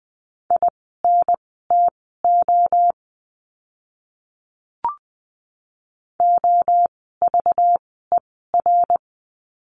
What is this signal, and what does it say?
Morse code, heard as a keyed tone, about 20 wpm, spaced as for 15 wpm, 700 Hz. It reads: INTO OVER
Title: 237 Lesson 38f - Letter q - Words 20wpm